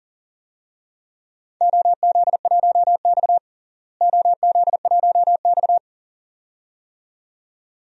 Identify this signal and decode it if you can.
Morse code, heard as a keyed tone, about 37 wpm, 700 Hz. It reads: OZ1X OZ1X
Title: DXCC-100 Call Signs - 4 Characters - Repeated 2x - 40wpm